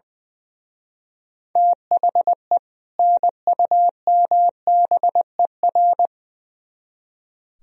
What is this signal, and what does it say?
Morse code, heard as a keyed tone, about 23 wpm, 700 Hz. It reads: THE NUMBER
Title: Speed-Racing - Sets of 2 Words 20wpm